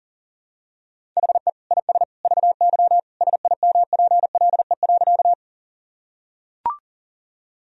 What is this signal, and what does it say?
Morse code, heard as a keyed tone, about 40 wpm, 700 Hz. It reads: HE IS VY SIMPLE.
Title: Sentences from Top 400 Words and Abbreviations 40wpm